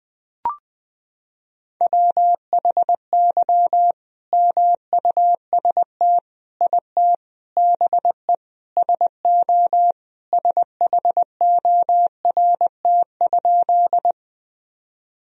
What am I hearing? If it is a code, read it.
Morse code, heard as a keyed tone, about 21 wpm, 700 Hz. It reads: WHY MUST IT BE SO SHORT?
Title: Sentences from Top 1000 Words - Review Part 2 - 20wpm